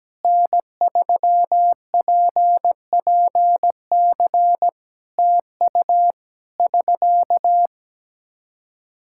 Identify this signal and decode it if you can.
Morse code, heard as a keyed tone, about 17 wpm, 700 Hz. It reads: N3PPC TU <SK>
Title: Practice QSO - Exchange 4 - Ending - 17wpm